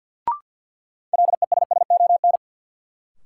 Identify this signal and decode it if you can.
Morse code, heard as a keyed tone, about 52 wpm, 700 Hz. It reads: LESSON
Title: Sentences from Top 1200 Words - New Word Drill - 50wpm